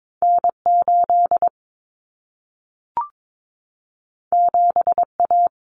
Morse code, heard as a keyed tone, about 22 wpm, 700 Hz. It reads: N8 7A
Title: Common Cut Numbers - 2-Digit 22wpm